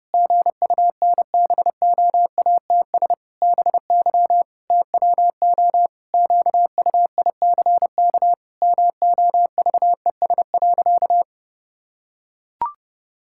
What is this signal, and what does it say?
Morse code, heard as a keyed tone, about 28 wpm, 700 Hz. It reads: GUNBOATS BY TWO QUICK MOVES.
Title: Pangram Challenge v2 - 30wpm